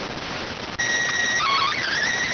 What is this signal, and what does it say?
SSTV (slow-scan television), PD 290.